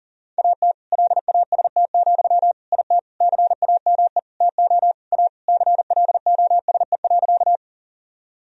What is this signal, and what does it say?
Morse code, heard as a keyed tone, about 37 wpm, 700 Hz. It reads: AT LAST, IT CAME TO A CLOSE.
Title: Sentences from Top 1000 Words - Review Part 1 - 40wpm